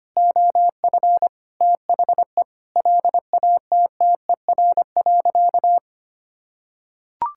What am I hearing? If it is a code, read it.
Morse code, heard as a keyed tone, about 25 wpm, 700 Hz. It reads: OF THE LATTER.
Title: Sentences from Top 1600 Words 25wpm